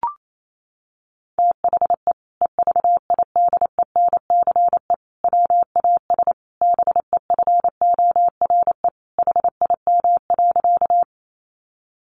Morse code, heard as a keyed tone, about 28 wpm, 700 Hz. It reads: THE EVIDENCE WAS BEFORE HIM.
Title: Sentences from Top 1300 Words 28wpm